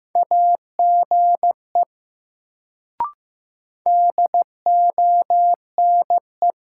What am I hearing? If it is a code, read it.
Morse code, heard as a keyed tone, about 15 wpm, 700 Hz. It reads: AGE DONE
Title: Sentences from Top 500 Words - New Word Drill - 15wpm